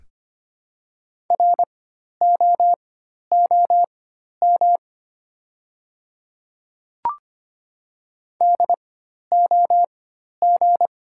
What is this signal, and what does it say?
Morse code, heard as a keyed tone, about 25 wpm, spaced as for 12 wpm, 700 Hz. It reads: ROOM DOG
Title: 4x Character Spacing - Top 500 Words - 25wpm